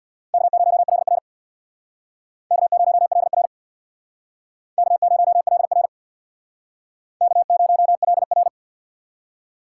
Morse code, heard as a keyed tone, about 68 wpm, 700 Hz. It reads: K0LR K0LR K0LR K0LR